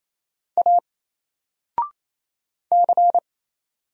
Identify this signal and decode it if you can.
Morse code, heard as a keyed tone, about 27 wpm, 700 Hz. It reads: A C
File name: Single-Letter-Number-28wpm